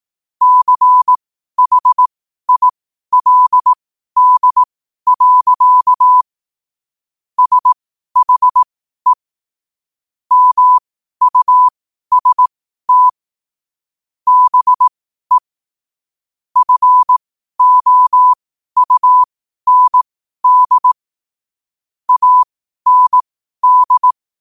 Morse code, heard as a keyed tone, about 18 wpm, spaced as for 12 wpm, 1000 Hz. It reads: CHILD. SHE MUST BE FOUND AND
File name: qotdfiles / 12wpm